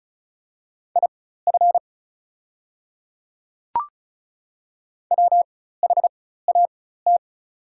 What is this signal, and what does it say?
Morse code, heard as a keyed tone, about 34 wpm, spaced as for 16 wpm, 700 Hz. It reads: IF WHAT
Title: 4x Character Spacing - Top 100 Words - 35wpm